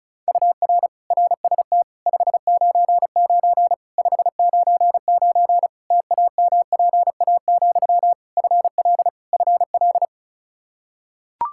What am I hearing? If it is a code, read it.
Morse code, heard as a keyed tone, about 36 wpm, 700 Hz. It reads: UR RST 599 599 TAMPA, FL FL
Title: K1USN SST QSO Practice 35wpm